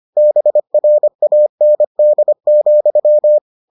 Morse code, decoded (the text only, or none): BRAND,